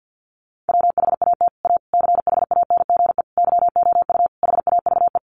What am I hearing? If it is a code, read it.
Morse code, heard as a keyed tone, about 48 wpm, 700 Hz. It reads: WHAT A CHANGE YOU HAVE
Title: Sentences from Top 600 Words 50wpm